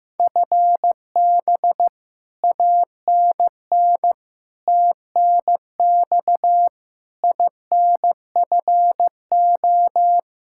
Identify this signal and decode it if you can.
Morse code, heard as a keyed tone, about 15 wpm, 700 Hz. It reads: FB ANN TNX INFO
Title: Practice QSO - Exchange 3 - More Info - 15wpm